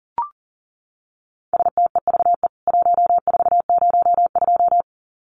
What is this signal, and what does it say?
Morse code, heard as a keyed tone, about 40 wpm, 700 Hz. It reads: STEVE 1402
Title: CWT Exchange 40wpm